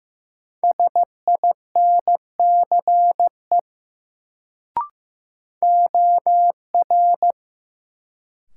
Morse code, heard as a keyed tone, about 16 wpm, 700 Hz. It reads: SINCE OR
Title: Sets of 2 Words - New Word Drill - 15wpm